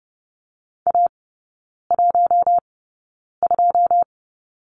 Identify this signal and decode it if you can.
Morse code, heard as a keyed tone, about 30 wpm, spaced as for 9 wpm, 700 Hz. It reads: A12
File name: 081 Lesson 14e - Number 2 - Three Characters 30wpm